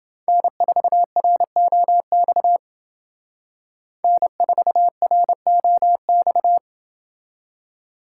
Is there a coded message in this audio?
Yes: Morse code from 0.3 to 6.6 s, heard as a keyed tone, about 28 wpm, 700 Hz. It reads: N4ROX N4ROX